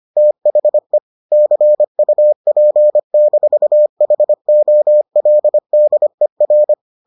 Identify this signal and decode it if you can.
Morse code, heard as a keyed tone, about 25 wpm, 600 Hz. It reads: THE CUP-HOLDER